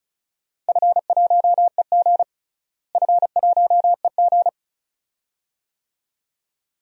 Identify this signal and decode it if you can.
Morse code, heard as a keyed tone, about 34 wpm, 700 Hz. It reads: F1EG F1EG
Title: DXCC-100 Call Signs - 4 Characters - Repeated 2x - 35wpm